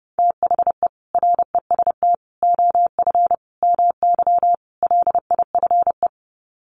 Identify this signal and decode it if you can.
Morse code, heard as a keyed tone, about 30 wpm, 700 Hz. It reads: THE REST OF MY LIFE
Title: Sets of 5 Words 30wpm